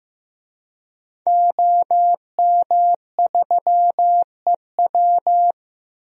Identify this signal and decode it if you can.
Morse code, heard as a keyed tone, about 15 wpm, 700 Hz. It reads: OM3EW
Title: DXCC-100 Call Signs - 5 Characters - 15wpm